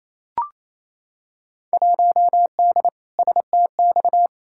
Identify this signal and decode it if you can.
Morse code, heard as a keyed tone, about 28 wpm, 700 Hz. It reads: 1D STX
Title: Field Day Exchanges Encore - 28wpm